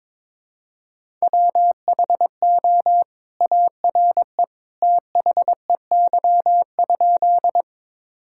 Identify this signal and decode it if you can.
Morse code, heard as a keyed tone, about 22 wpm, 700 Hz. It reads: WHO ARE THEY?
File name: practice / 22wpm / Sentences with 2-Word Phrases - 22wpm